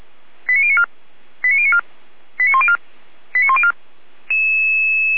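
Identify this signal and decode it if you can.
Five-tone selective call (ZVEI), heard as quick runs of five tones, tones of about 70 ms. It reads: 89094 89094 89104 89104
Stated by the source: ZVEI five-tone selective call received by SDR, from the feuerwehr-orlamuende.org audio archive.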